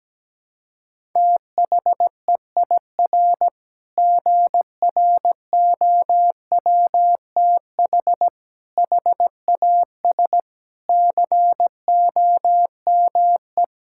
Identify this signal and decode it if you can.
Morse code, heard as a keyed tone, about 17 wpm, 700 Hz. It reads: THEIR GROWTH HAS COME
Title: Sentences from Top 1000 Words - Review Part 5 - 17wpm